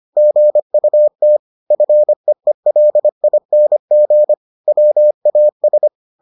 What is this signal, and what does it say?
Morse code, heard as a keyed tone, about 25 wpm, 600 Hz. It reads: GUT FEELING WAS